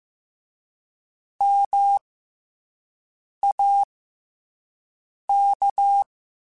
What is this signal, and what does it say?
Morse code, heard as a keyed tone, about 15 wpm, spaced as for 5 wpm, 800 Hz. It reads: MAK
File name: Special INGs-mp3s / Special INGs-mp3s / SlowToFaster ing @10